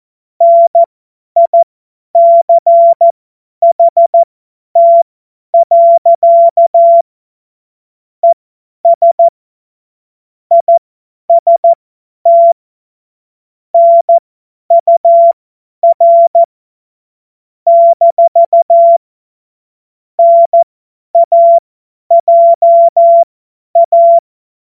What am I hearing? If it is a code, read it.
Morse code, heard as a keyed tone, about 14 wpm, spaced as for 10 wpm, 680 Hz. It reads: NICHT. ES IST NUR - NAJA